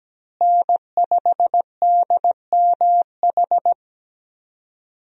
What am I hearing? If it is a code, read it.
Morse code, heard as a keyed tone, about 17 wpm, 700 Hz. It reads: N5DMH
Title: USA Call Signs - 5 Characters - Repeated 2x - 17wpm